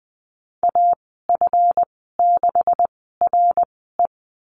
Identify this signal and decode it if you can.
Morse code, heard as a keyed tone, about 20 wpm, spaced as for 15 wpm, 700 Hz. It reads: AF6RE
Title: 161 Lesson 26g - Letter g - Call Signs 20wpm